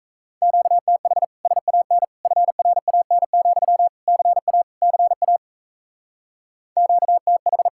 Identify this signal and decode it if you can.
Morse code, heard as a keyed tone, about 35 wpm, 700 Hz. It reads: QTH SAN FRAN, CA CA QTH
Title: Speed-Racing Common QSO Phrases Encore - 28wpm